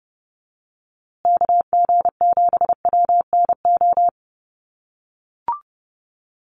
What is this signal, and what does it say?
Morse code, heard as a keyed tone, about 30 wpm, 700 Hz. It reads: KG7WNO
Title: USA Call Signs Repeated 2x 30wpm